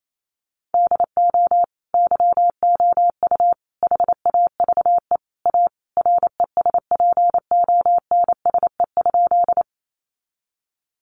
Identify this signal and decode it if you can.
Morse code, heard as a keyed tone, about 28 wpm, 700 Hz. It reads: DO YOU HAVE A RESPONSE?
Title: Sentences from Top 1400 Words 28wpm